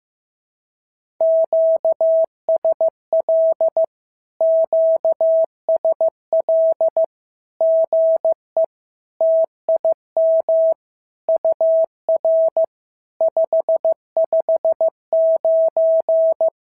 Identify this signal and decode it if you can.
Morse code, heard as a keyed tone, about 15 wpm, 650 Hz. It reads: QSL QSL GE TIM UR 559
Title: SOTA QSO Practice 15wpm